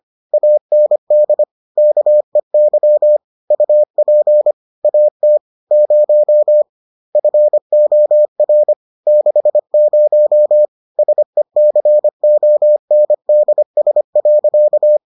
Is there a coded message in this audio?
Yes: Morse code from 0.3 to 15.0 s, heard as a keyed tone, about 25 wpm, 600 Hz. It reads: AND KEY UP AT 0 FOR 60 SECONDS.